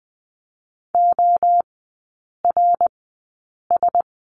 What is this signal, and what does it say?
Morse code, heard as a keyed tone, about 20 wpm, spaced as for 8 wpm, 700 Hz. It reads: ORS
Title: 053 Lesson 10e - Letter r - Three Characters 20wpm